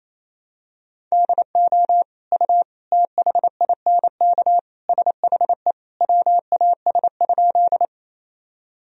Morse code, heard as a keyed tone, about 28 wpm, 700 Hz. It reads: DO U THINK SHE WAS?
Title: Sentences from Top 200 Words and Abbreviations 28wpm